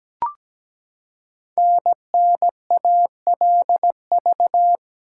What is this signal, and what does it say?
Morse code, heard as a keyed tone, about 17 wpm, 700 Hz. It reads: NNALV
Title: Code Groups - 5 Letter:Numbers - 17wpm